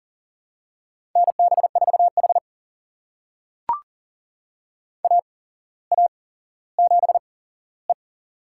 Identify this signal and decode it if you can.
Morse code, heard as a keyed tone, about 40 wpm, spaced as for 17 wpm, 700 Hz. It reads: NB4H AA7E
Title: USA Call Signs - 4 Characters - 8x Character Spacing - 40wpm